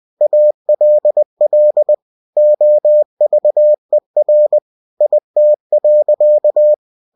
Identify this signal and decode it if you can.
Morse code, heard as a keyed tone, about 20 wpm, 600 Hz. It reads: ALL OVER IT.
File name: cw-books / winnie-the-pooh / winnie-20wpm-page-0093